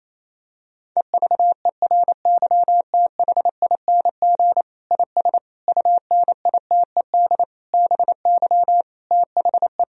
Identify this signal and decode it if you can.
Morse code, heard as a keyed tone, about 28 wpm, 700 Hz. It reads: EVERYTHING IS UNITED BY THE
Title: Sentences from Top 1000 Words - Review Part 4 - 28wpm